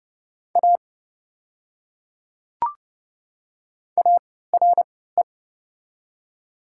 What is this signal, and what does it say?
Morse code, heard as a keyed tone, about 31 wpm, spaced as for 17 wpm, 690 Hz. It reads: A ARE
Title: 3x Character Spacing - Top 200 Words - 30wpm